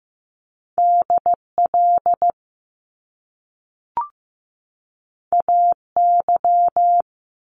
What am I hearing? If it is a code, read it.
Morse code, heard as a keyed tone, about 15 wpm, 700 Hz. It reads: DL AY
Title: Top Two Letter Combinations in Words 15wpm